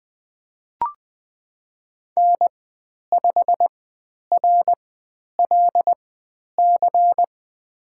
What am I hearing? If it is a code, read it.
Morse code, heard as a keyed tone, about 20 wpm, spaced as for 10 wpm, 700 Hz. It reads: N5RLC
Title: 102 Lesson 17g - Letter u - Call Signs 20wpm